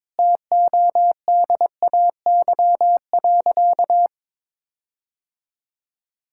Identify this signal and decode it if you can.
Morse code, heard as a keyed tone, about 21 wpm, 700 Hz. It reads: TODAY.
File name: practice / 22wpm / Sentences with 4-Word Phrases - 22wpm